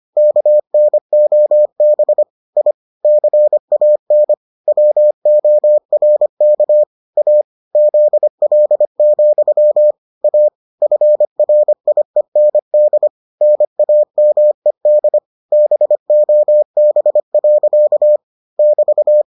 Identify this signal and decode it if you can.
Morse code, heard as a keyed tone, about 25 wpm, 600 Hz. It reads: KNOB I CAN WORK A ZL, A FRIEND NAMED BOB. <BT>